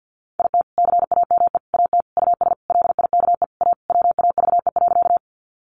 Morse code, heard as a keyed tone, about 50 wpm, 700 Hz. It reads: IT CAME AT US LIKE A WAVE.
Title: Sentences from Top 1400 Words 50wpm